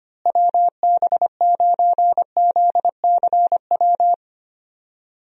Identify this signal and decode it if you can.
Morse code, heard as a keyed tone, about 25 wpm, 700 Hz. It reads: WB9ZCW
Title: USA Call Signs - 6 Characters - Repeated 2x - 25wpm